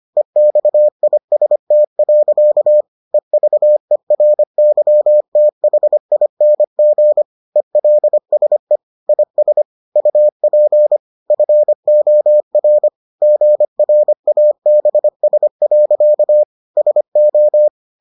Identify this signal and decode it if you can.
Morse code, heard as a keyed tone, about 25 wpm, 600 Hz. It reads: EXIST. EVERYTHING ELSE IS UP FOR GRABS. SO